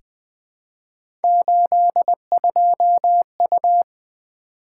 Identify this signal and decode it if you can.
Morse code, heard as a keyed tone, about 20 wpm, 700 Hz. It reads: 82U
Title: Three Letter-Number Combinations - 20wpm